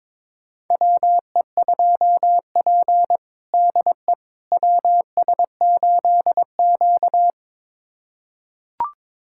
Morse code, heard as a keyed tone, about 21 wpm, 700 Hz. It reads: WE2P DE WS8Q
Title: USA Callsign DE Callsign - 4 Characters - 22wpm